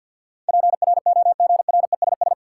Morse code, heard as a keyed tone, about 53 wpm, 700 Hz. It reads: PROGRESS
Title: Sentences from Top 1400 Words - New Word Drill Alt - 50wpm